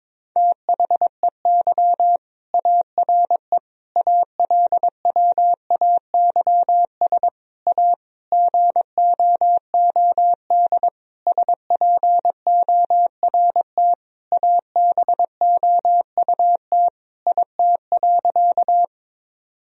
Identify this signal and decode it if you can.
Morse code, heard as a keyed tone, about 22 wpm, 700 Hz. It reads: THEY ARE ALWAYS A GOOD SPORT ABOUT IT.
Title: Sentences from Top 1000 Words - Review Part 4 - 22wpm